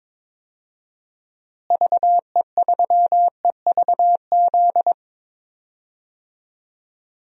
Morse code, heard as a keyed tone, about 22 wpm, 700 Hz. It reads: VE3EVZ
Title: DXCC-100 Call Signs - 6 Characters - Repeated 3x - 22wpm